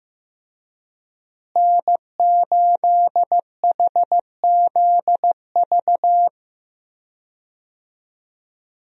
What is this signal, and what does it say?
Morse code, heard as a keyed tone, about 15 wpm, 700 Hz. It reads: N8HZV
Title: USA Call Signs - 5 Characters - Repeated 3x - 15wpm